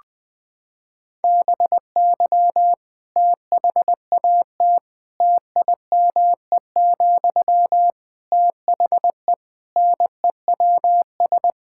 Morse code, heard as a keyed tone, about 21 wpm, 700 Hz. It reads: BY THAT TIME, THE NEWS
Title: Sentences from Top 1600 Words 20wpm